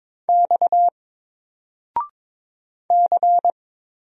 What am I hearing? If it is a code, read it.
Morse code, heard as a keyed tone, about 22 wpm, 700 Hz. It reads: X C
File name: Single-Letter-Number-22wpm